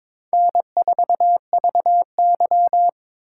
Morse code, heard as a keyed tone, about 22 wpm, 700 Hz. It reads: N4VY
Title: USA Call Signs - 4 Characters - 2x Character Spacing - 22wpm